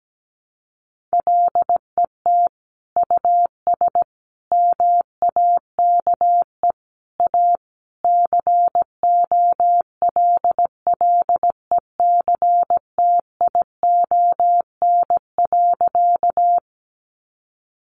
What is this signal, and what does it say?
Morse code, heard as a keyed tone, about 17 wpm, 700 Hz. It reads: LET US MAKE A COLLECTION.
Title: Sentences from Top 1200 Words 17wpm